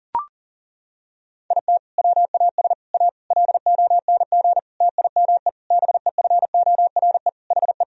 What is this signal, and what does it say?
Morse code, heard as a keyed tone, about 40 wpm, 700 Hz. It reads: IT WAS A LONG TIME BEFORE HE
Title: Sentences from Top 1000 Words - Review Part 2 - 40wpm